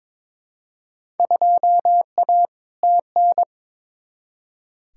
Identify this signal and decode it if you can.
Morse code, heard as a keyed tone, about 22 wpm, 700 Hz. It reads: 2A TN
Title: Field Day Exchanges - 22wpm